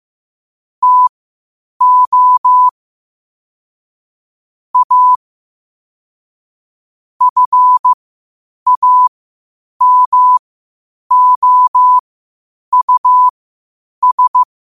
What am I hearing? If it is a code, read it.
Morse code, heard as a keyed tone, about 15 wpm, spaced as for 8 wpm, 1000 Hz. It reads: TO A FAMOUS
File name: qotdfiles / 8wpm